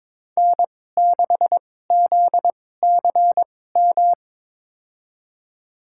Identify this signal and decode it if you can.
Morse code, heard as a keyed tone, about 22 wpm, spaced as for 16 wpm, 700 Hz. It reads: N6ZCM
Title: USA Call Signs - 5 Characters - 2x Character Spacing - 22wpm